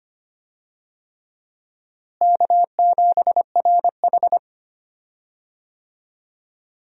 Morse code, heard as a keyed tone, about 25 wpm, 700 Hz. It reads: K7RH
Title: USA Call Signs - 4 Characters - Repeated 3x - 25wpm